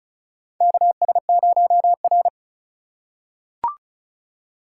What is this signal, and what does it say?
Morse code, heard as a keyed tone, about 36 wpm, 700 Hz. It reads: KS0R
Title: USA Call Signs - 2x Character Spacing - 35wpm